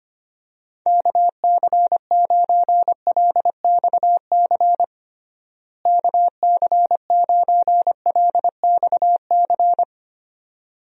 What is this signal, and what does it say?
Morse code, heard as a keyed tone, about 25 wpm, 700 Hz. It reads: KC9LXC KC9LXC
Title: USA Call Signs - 6 Characters - Repeated 2x - 25wpm